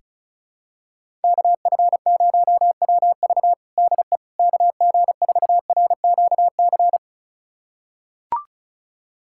Morse code, heard as a keyed tone, about 36 wpm, 700 Hz. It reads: KF0WV DE KG4RQC
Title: USA Callsign DE Callsign 35wpm